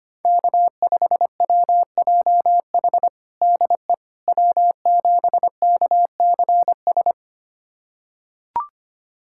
Morse code, heard as a keyed tone, about 25 wpm, 700 Hz. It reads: K5WJH DE W7KCS
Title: USA Callsign DE Callsign - 5 Characters - 25wpm